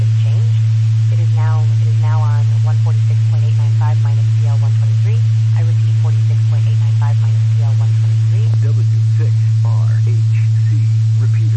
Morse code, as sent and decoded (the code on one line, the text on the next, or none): none